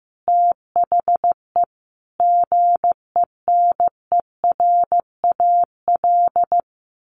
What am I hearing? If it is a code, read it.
Morse code, heard as a keyed tone, about 15 wpm, 700 Hz. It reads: THE GENERAL
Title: Sentences from Top 1400 Words 15wpm